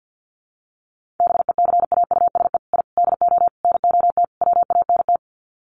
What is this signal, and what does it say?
Morse code, heard as a keyed tone, about 50 wpm, 700 Hz. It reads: BECAUSE I DO NOT WANT
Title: Sets of 5 Words Encore 50wpm